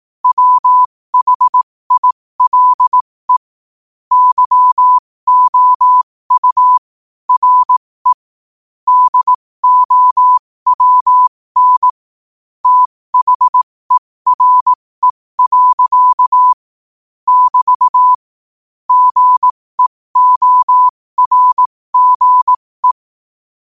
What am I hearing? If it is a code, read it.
Morse code, heard as a keyed tone, about 18 wpm, spaced as for 15 wpm, 1000 Hz. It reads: WHILE YOU RE DOWN THERE. <BT> GEORGE
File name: qotdfiles / 15wpm